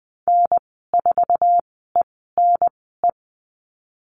Morse code, heard as a keyed tone, about 20 wpm, spaced as for 14 wpm, 700 Hz. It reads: N4ENE
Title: 048 Lesson 9g - Number 4 - Call Signs 20wpm